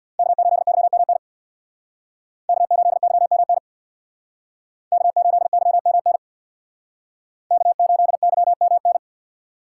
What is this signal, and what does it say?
Morse code, heard as a keyed tone, about 62 wpm, 700 Hz. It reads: K8YMN K8YMN K8YMN K8YMN
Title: Speed Racing - USA 1x3 Call Signs 50wpm